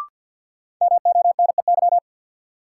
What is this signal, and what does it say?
Morse code, heard as a keyed tone, about 53 wpm, 700 Hz. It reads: MONEY
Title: Sentences from Top 400 Words - New Word Drill - 50wpm